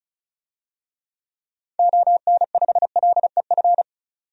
Morse code, heard as a keyed tone, about 34 wpm, 700 Hz. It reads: ON5LEF
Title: DXCC-100 Call Signs - 6 Characters - 35wpm